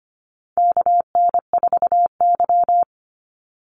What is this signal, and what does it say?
Morse code, heard as a keyed tone, about 25 wpm, 700 Hz. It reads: KN4Y
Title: Call Signs Repeated 3x - 25wpm